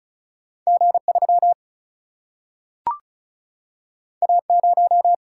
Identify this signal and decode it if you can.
Morse code, heard as a keyed tone, about 36 wpm, 700 Hz. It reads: G 3 A 0
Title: Single-Letter Single-Number 35wpm